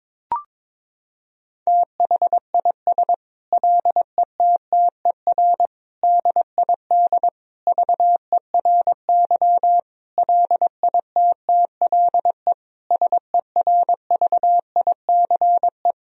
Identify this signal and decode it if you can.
Morse code, heard as a keyed tone, about 22 wpm, 700 Hz. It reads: THIS LETTER DID VERY LITTLE SERVICE
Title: Sentences from Top 1000 Words - Review Part 4 - 22wpm